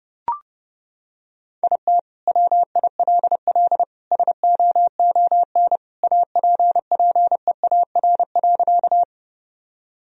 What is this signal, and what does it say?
Morse code, heard as a keyed tone, about 32 wpm, 700 Hz. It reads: IT WILL SOON APPEAR.
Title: Sentences from Top 1000 Words - Review Part 2 - 30wpm